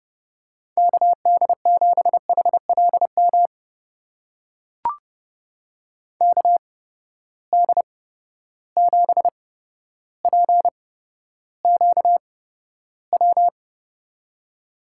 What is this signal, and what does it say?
Morse code, heard as a keyed tone, about 30 wpm, spaced as for 13 wpm, 700 Hz. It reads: KD7HLM KD7PQW
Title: USA Call Signs - 6 Characters - 8x Character Spacing - 30wpm